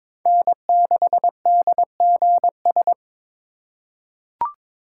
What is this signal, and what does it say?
Morse code, heard as a keyed tone, about 22 wpm, 700 Hz. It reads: N6DGS
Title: USA Call Signs - 5 Characters - 3x Character Spacing - 22wpm